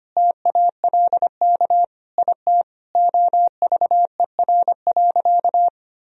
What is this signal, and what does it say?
Morse code, heard as a keyed tone, about 25 wpm, 700 Hz. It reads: TALK IT OVER.
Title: Sentences from Top 500 Words and Abbreviations 25wpm